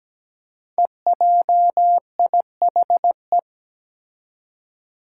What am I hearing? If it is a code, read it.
Morse code, heard as a keyed tone, about 17 wpm, 700 Hz. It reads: EJIHE
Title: Five Challenging Letter-Number Combinations 17wpm